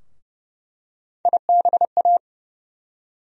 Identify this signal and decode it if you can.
Morse code, heard as a keyed tone, about 31 wpm, 700 Hz. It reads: IBA
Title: USA Call Sign Postfixes - 3 Letters 30wpm